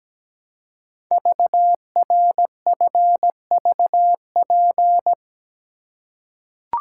Morse code, heard as a keyed tone, about 17 wpm, 700 Hz. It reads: VRFVP
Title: Code Groups - 5 Letters - 17wpm